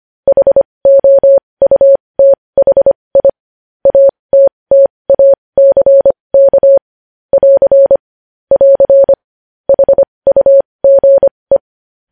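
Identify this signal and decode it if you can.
Morse code, heard as a keyed tone, about 25 wpm, spaced as for 20 wpm, 550 Hz. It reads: HOUTHI ATTACK <AR> <AR> HUGE
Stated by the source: Morse practice sending news headlines at 20 wpm.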